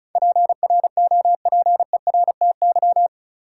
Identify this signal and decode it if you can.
Morse code, heard as a keyed tone, about 36 wpm, 700 Hz. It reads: PROPERTY
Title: Sentences from Top 1000 Words - New Word Drill - 35wpm